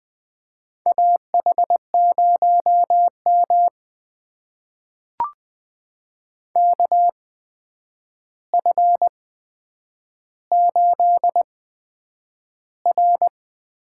Morse code, heard as a keyed tone, about 20 wpm, spaced as for 9 wpm, 700 Hz. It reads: AH0M KF8R
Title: USA Call Signs - 4 Characters - 8x Character Spacing - 20wpm